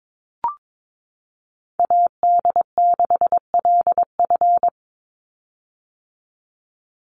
Morse code, heard as a keyed tone, about 22 wpm, 700 Hz. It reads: AD6LF
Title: USA Call Signs 22wpm